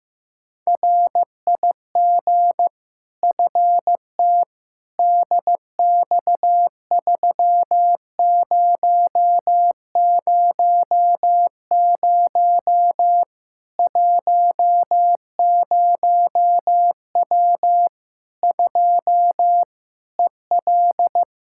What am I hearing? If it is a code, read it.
Morse code, heard as a keyed tone, about 16 wpm, 700 Hz. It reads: RIG FT DX3000 10W 2 EL
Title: Practice QSO - Exchange 2 - Info - 15wpm